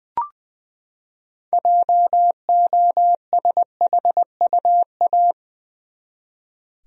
Morse code, heard as a keyed tone, about 20 wpm, 700 Hz. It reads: JOSHUA
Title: USA Common Names - 20wpm